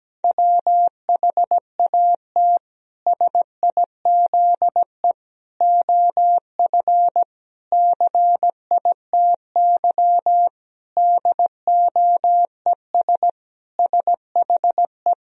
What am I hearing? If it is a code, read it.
Morse code, heard as a keyed tone, about 17 wpm, 700 Hz. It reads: WHAT SIZE OF CITY DOES SHE
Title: Sentences from Top 1000 Words - Review Part 2 - 17wpm